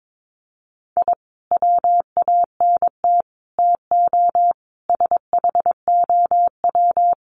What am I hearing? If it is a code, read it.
Morse code, heard as a keyed tone, about 22 wpm, 700 Hz. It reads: I WANT TO SHOW
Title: Sets of 5 Words 22wpm